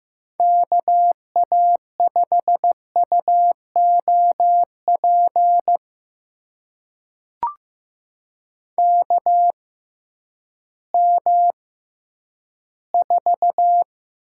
Morse code, heard as a keyed tone, about 15 wpm, spaced as for 10 wpm, 700 Hz. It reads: KA5UOP KM4
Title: USA Call Signs - 6 Characters - 6x Character Spacing - 15wpm